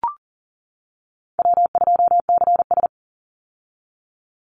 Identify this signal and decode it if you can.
Morse code, heard as a keyed tone, about 40 wpm, 700 Hz. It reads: W2CS
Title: Easy Call Signs - 40wpm